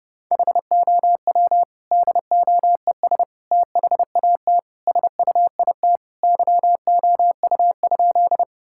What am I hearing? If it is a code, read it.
Morse code, heard as a keyed tone, about 30 wpm, 700 Hz. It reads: HOW DOES THAT SUIT YOU?
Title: Sentences from Top 1600 Words 30wpm